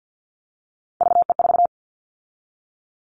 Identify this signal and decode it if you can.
Morse code, heard as a keyed tone, about 50 wpm, 700 Hz. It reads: VE4
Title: Callsign Prefixes for Top 50 DXCC - 50wpm